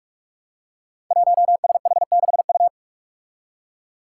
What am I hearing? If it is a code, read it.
Morse code, heard as a keyed tone, about 45 wpm, 700 Hz. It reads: 1SH6U
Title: Five Challenging Letter-Number Combinations 45wpm